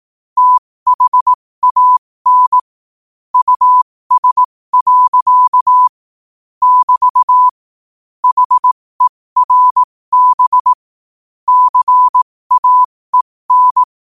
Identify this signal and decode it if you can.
Morse code, heard as a keyed tone, about 18 wpm, spaced as for 15 wpm, 1000 Hz. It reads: THAN US. <BT> HERB CAEN
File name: qotdfiles / 15wpm